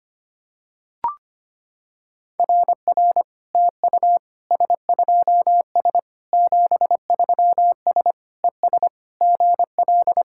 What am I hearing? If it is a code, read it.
Morse code, heard as a keyed tone, about 25 wpm, 700 Hz. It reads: RR TU S2S 73S ES GL
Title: SOTA QSO Practice 25wpm